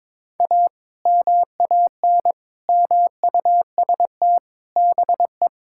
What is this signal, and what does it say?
Morse code, heard as a keyed tone, about 22 wpm, 700 Hz. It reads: A MAN MUST BE
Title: Sentences from Top 1000 Words - Review Part 6 - 22wpm